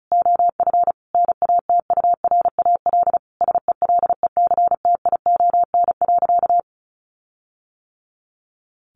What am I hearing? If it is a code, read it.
Morse code, heard as a keyed tone, about 34 wpm, 700 Hz. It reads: OF NATURAL SELECTION.
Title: Sentences from Top 1400 Words 35wpm